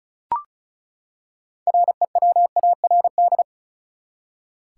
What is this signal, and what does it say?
Morse code, heard as a keyed tone, about 36 wpm, 700 Hz. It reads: REWARD
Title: Six Letter Words - 35wpm